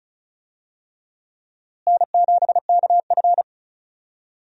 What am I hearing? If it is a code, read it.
Morse code, heard as a keyed tone, about 36 wpm, 700 Hz. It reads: N7KF
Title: USA Call Signs - 4 Characters - Repeated 3x - 35wpm